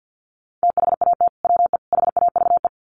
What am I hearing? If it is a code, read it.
Morse code, heard as a keyed tone, about 48 wpm, 700 Hz. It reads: THAT WE HAVE
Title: Sets of 3 Words 50wpm